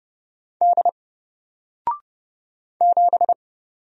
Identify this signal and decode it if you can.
Morse code, heard as a keyed tone, about 29 wpm, 700 Hz. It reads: D 7
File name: Single-Letter-Number-30wpm